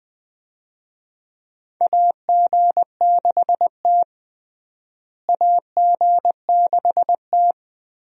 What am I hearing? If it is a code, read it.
Morse code, heard as a keyed tone, about 20 wpm, 700 Hz. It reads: AG6T AG6T
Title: USA Call Signs - 4 Characters - Repeated 2x - 20wpm